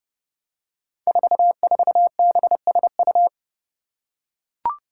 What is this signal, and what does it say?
Morse code, heard as a keyed tone, about 32 wpm, 700 Hz. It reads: 44BSU
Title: Five Challenging Letter-Number Combinations 30wpm